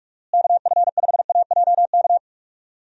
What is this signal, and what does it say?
Morse code, heard as a keyed tone, about 46 wpm, 700 Hz. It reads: KF5AJK
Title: USA Call Signs - 6 Characters - Repeated 2x - 45wpm